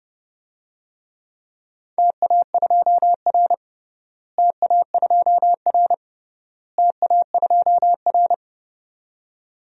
Morse code, heard as a keyed tone, about 30 wpm, 700 Hz. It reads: TA2R TA2R TA2R
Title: DXCC-100 Call Signs - 4 Characters - Repeated 3x - 30wpm